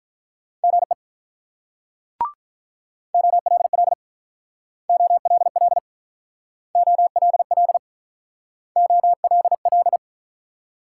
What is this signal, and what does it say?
Morse code, heard as a keyed tone, about 48 wpm, 700 Hz. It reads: GE OLL OLL OLL OLL